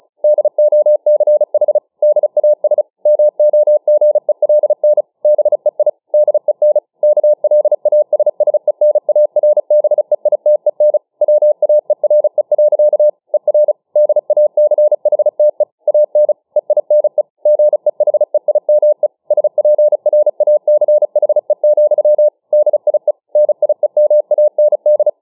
Morse code, heard as a keyed tone, about 34 wpm, 600 Hz. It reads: DOCH DAS MOGELN BEI DEN KLASSENARBEITEN WAERE. ER DACHTE AN EINE GEHEIME SPRACHE, DIE NIEMAND